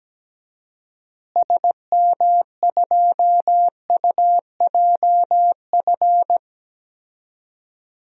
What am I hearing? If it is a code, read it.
Morse code, heard as a keyed tone, about 17 wpm, 700 Hz. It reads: SM2UJF
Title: DXCC-100 Call Signs - 6 Characters - Repeated 2x - 17wpm